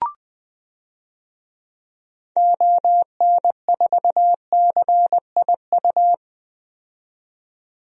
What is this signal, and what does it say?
Morse code, heard as a keyed tone, about 20 wpm, 700 Hz. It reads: ON4CIU
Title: DXCC-100 Call Signs - 6 Characters - 20wpm